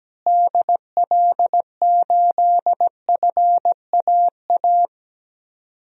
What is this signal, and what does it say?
Morse code, heard as a keyed tone, about 17 wpm, 700 Hz. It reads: DL8FAA
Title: DXCC-100 Call Signs - 6 Characters - Repeated 2x - 17wpm